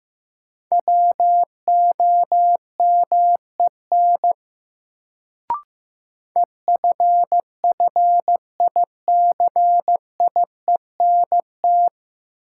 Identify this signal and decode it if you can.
Morse code, heard as a keyed tone, about 15 wpm, 700 Hz. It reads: WOMEN EFFICIENT
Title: Sentences from Top 1300 Words - New Word Drill - 15wpm